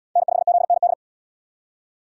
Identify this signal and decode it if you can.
Morse code, heard as a keyed tone, about 72 wpm, 700 Hz. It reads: N5ZND